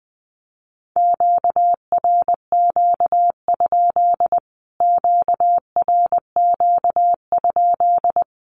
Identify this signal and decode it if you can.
Morse code, heard as a keyed tone, about 19 wpm, 700 Hz. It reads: QRQ? QRQ?
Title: Q-Codes Repeated 3x 20wpm